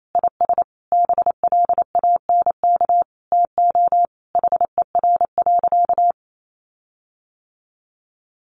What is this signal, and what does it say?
Morse code, heard as a keyed tone, about 28 wpm, 700 Hz. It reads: IS BLANK TO HER.
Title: Sentences from Top 1500 Words 28wpm